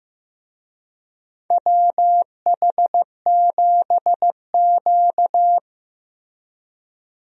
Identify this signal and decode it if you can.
Morse code, heard as a keyed tone, about 15 wpm, 700 Hz. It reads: WH7Q
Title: USA Call Signs - 4 Characters - Repeated 3x - 15wpm